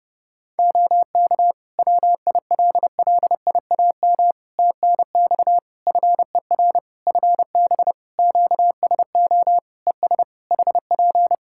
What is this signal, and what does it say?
Morse code, heard as a keyed tone, about 30 wpm, 700 Hz. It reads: OK WILLIAM TNX FER FB QSO ES HP
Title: Practice QSO - Exchange 4 - Ending - 30wpm